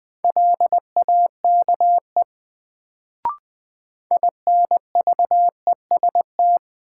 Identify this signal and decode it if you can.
Morse code, heard as a keyed tone, about 21 wpm, 700 Hz. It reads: LAKE INVEST
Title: Sentences from Top 1100 Words - New Word Drill Alt - 20wpm